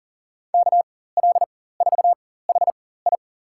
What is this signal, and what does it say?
Morse code, heard as a keyed tone, about 40 wpm, spaced as for 19 wpm, 700 Hz. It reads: KL4HI
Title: USA Call Signs - 5 Characters - 4x Character Spacing - 40wpm